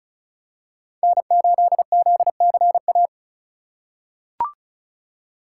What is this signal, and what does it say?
Morse code, heard as a keyed tone, about 36 wpm, 700 Hz. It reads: N8ZCA
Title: USA Call Signs - 5 Characters - 2x Character Spacing - 35wpm